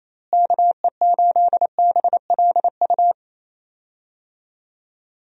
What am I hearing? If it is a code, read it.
Morse code, heard as a keyed tone, about 27 wpm, 700 Hz. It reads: KE8BLU
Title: USA Call Signs - 6 Characters - Repeated 3x - 28wpm